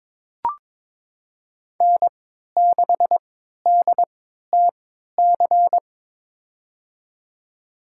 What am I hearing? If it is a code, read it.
Morse code, heard as a keyed tone, about 21 wpm, spaced as for 12 wpm, 700 Hz. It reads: N6DTC
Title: USA Call Signs - 3x Character Spacing - 22wpm